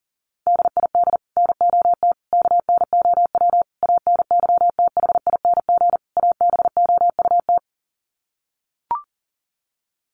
Morse code, heard as a keyed tone, about 40 wpm, 700 Hz. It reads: DID NOT KNOW ANYTHING ABOUT
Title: Sets of 5 Words Encore2 40wpm